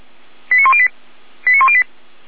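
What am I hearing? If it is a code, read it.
Five-tone selective call (ZVEI), heard as quick runs of five tones, tones of about 70 ms. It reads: 89108 89108
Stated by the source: ZVEI five-tone selective call received on an SDR, from the Feuerwehr Orlamünde audio archive.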